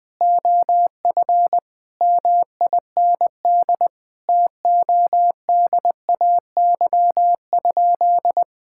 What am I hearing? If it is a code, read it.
Morse code, heard as a keyed tone, about 20 wpm, 700 Hz. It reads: OF MIND TODAY?
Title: Sentences from Top 1000 Words - Review Part 6 - 20wpm